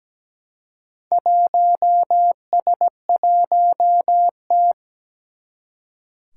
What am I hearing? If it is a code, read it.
Morse code, heard as a keyed tone, about 17 wpm, 700 Hz. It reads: 1S1T
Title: Four Challenging Letter-Number Combinations 17wpm